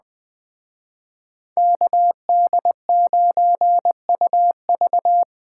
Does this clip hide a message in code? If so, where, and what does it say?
Yes: Morse code from 1.6 to 5.2 s, heard as a keyed tone, about 19 wpm, 700 Hz. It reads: KD9UV